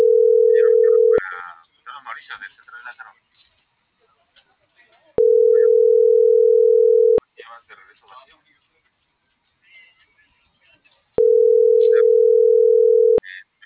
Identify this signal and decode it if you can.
Ringback tone (North America), 440 + 480 Hz, with speech.
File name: CENTRAL TERRESTRE